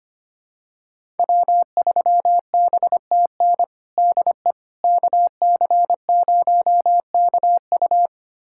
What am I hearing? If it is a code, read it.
Morse code, heard as a keyed tone, about 25 wpm, 700 Hz. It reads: W3BTN DE KC0KU
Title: USA Callsign DE Callsign - 5 Characters - 25wpm